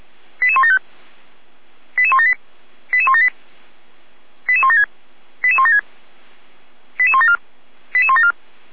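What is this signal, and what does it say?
Five-tone selective call (ZVEI), heard as quick runs of five tones, tones of about 70 ms. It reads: 88175 88178 88178 88176 88176 88174 88174
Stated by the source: Five-tone ZVEI selective call received on an SDR.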